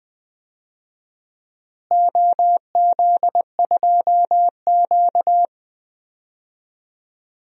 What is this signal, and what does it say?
Morse code, heard as a keyed tone, about 20 wpm, 700 Hz. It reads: OZ2Q
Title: DXCC-100 Call Signs - 4 Characters - Repeated 2x - 20wpm